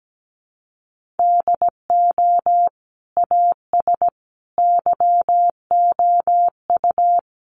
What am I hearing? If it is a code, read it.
Morse code, heard as a keyed tone, about 17 wpm, 700 Hz. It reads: DO AS YOU
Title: Sentences from Top 100 Words 17wpm